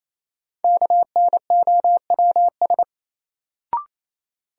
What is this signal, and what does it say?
Morse code, heard as a keyed tone, about 28 wpm, 700 Hz. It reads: KNOWS
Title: Sentences from Top 1000 Words - Review Part 1 - New Word Drill - 28wpm